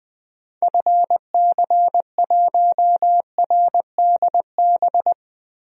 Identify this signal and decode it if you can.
Morse code, heard as a keyed tone, about 19 wpm, 700 Hz. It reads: FC1RDB
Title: DXCC-100 Call Signs - 6 Characters - Repeated 3x - 20wpm